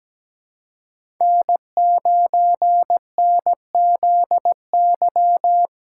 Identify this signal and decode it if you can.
Morse code, heard as a keyed tone, about 17 wpm, 700 Hz. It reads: N9NZY
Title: USA Call Signs - 5 Characters - 2x Character Spacing - 17wpm